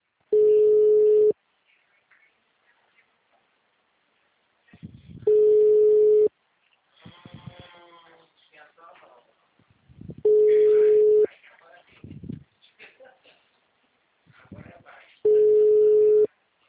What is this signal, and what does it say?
Ringback tone (Europe, 425 Hz), with speech.